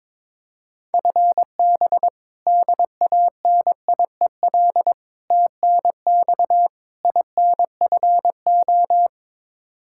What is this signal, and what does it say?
Morse code, heard as a keyed tone, about 21 wpm, 700 Hz. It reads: FB DANIEL TNX INFO
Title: Practice QSO - Exchange 3 - More Info - 22wpm